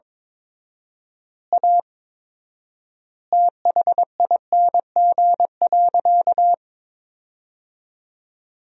Morse code, heard as a keyed tone, about 22 wpm, 700 Hz. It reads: A THING.
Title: 4x Word-Spacing - Sentences from Top 200 Words - 22wpm